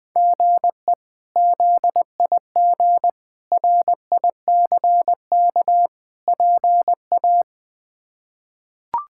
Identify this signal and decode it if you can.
Morse code, heard as a keyed tone, about 21 wpm, 700 Hz. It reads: GE ZIG RICK PA
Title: K1USN SST QSO Practice - 20wpm